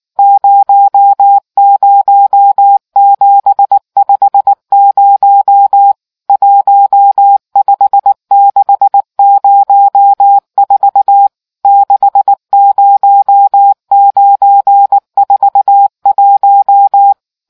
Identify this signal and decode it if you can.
Morse code, heard as a keyed tone, about 19 wpm, 800 Hz. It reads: 00750 15604 60941